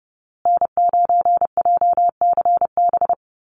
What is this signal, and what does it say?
Morse code, heard as a keyed tone, about 30 wpm, 700 Hz. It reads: N9JCB
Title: USA Call Signs Repeated 2x 30wpm